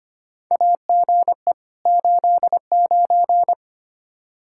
Morse code, heard as a keyed tone, about 25 wpm, 700 Hz. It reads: AGE 89
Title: K1USN SST QSO Practice 25wpm